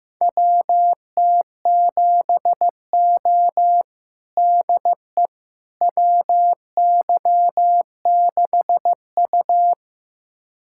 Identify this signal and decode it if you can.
Morse code, heard as a keyed tone, about 15 wpm, 700 Hz. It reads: WT7O DE WY6U
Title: USA Callsign DE Callsign - 4 Characters - 15wpm